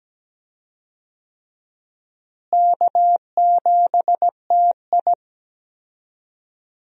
Morse code, heard as a keyed tone, about 17 wpm, 700 Hz. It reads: K7TI
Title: USA Call Signs - 4 Characters - Repeated 3x - 17wpm